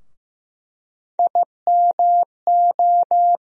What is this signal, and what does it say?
Morse code, heard as a keyed tone, about 15 wpm, 700 Hz. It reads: IMO
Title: USA Call Sign Postfixes - 3 Letters 15wpm